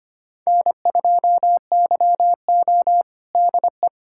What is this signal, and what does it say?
Morse code, heard as a keyed tone, about 25 wpm, 700 Hz. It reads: N2YO DE
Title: USA Callsign DE Callsign - 4 Characters - 25wpm